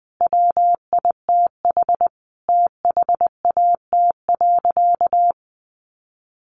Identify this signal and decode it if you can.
Morse code, heard as a keyed tone, about 20 wpm, 700 Hz. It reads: WITH THAT.
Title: Sentences from Top 900 Words 20wpm